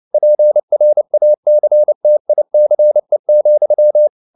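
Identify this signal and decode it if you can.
Morse code, heard as a keyed tone, about 29 wpm, 600 Hz. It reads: PRACTICE,
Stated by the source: Morse code at 29 wpm, sending Ken Follett's novel Never in English, from CW med Gnister.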